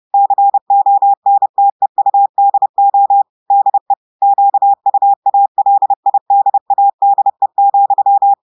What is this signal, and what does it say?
Morse code, heard as a keyed tone, about 30 wpm, 800 Hz. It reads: CONTEUDO DE QUALIDADE,
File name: audios / portugais / portugais texte en clair 2 vitesse 30 mots minutes